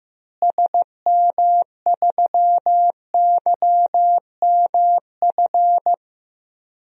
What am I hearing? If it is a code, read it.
Morse code, heard as a keyed tone, about 16 wpm, 700 Hz. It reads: SM3YMF
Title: DXCC-100 Call Signs - 6 Characters - Repeated 2x - 15wpm